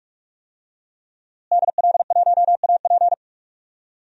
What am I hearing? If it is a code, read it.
Morse code, heard as a keyed tone, about 46 wpm, 700 Hz. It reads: DL1AP
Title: DXCC-100 Call Signs - 5 Characters - Repeated 2x - 45wpm